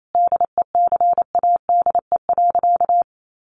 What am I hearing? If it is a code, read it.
Morse code, heard as a keyed tone, about 28 wpm, 700 Hz. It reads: DECADE.
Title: A Sound Of Thunder 28wpm